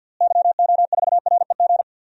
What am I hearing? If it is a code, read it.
Morse code, heard as a keyed tone, about 51 wpm, 700 Hz. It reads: YO4REG
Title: DXCC-100 Call Signs - 6 Characters - Repeated 3x - 50wpm